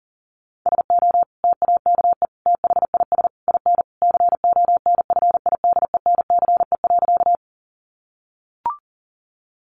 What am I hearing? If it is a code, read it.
Morse code, heard as a keyed tone, about 40 wpm, 700 Hz. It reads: SO TAKE THIS IN CONFIDENCE.
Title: Sentences from Top 1500 Words 40wpm